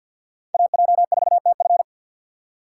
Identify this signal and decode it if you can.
Morse code, heard as a keyed tone, about 51 wpm, 700 Hz. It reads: AJ4TF
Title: K1USN SST QSO Practice - 50wpm